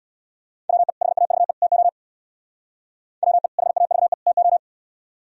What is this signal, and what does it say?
Morse code, heard as a keyed tone, about 64 wpm, 700 Hz. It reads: WE HAVE TO WE HAVE TO